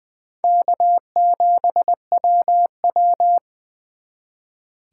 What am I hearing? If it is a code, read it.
Morse code, heard as a keyed tone, about 23 wpm, 700 Hz. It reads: K7WW
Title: Speed Racing - USA 1x2 Call Signs 20wpm